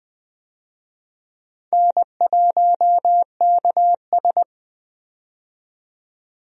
Morse code, heard as a keyed tone, about 20 wpm, 700 Hz. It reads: N1KS
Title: USA Call Signs - 4 Characters - 20wpm